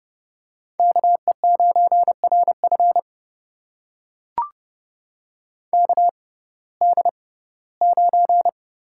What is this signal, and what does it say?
Morse code, heard as a keyed tone, about 30 wpm, spaced as for 18 wpm, 700 Hz. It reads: KE9RF KD9
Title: USA Call Signs - 5 Characters - 6x Character Spacing - 30wpm